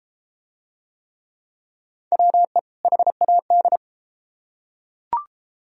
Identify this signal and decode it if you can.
Morse code, heard as a keyed tone, about 30 wpm, 700 Hz. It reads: WE HAD